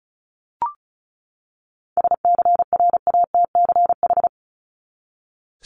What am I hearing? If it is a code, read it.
Morse code, heard as a keyed tone, about 34 wpm, 700 Hz. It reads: SCRATCH
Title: Top 1501-1750 Words 35wpm